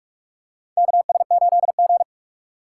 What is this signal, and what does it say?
Morse code, heard as a keyed tone, about 45 wpm, 700 Hz. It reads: KS8G
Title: USA Call Signs - 4 Characters - 6x Character Spacing - 45wpm